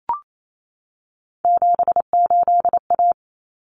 Morse code, heard as a keyed tone, about 28 wpm, 700 Hz. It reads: 78A
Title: Common Cut Numbers - 3-Digit 28wpm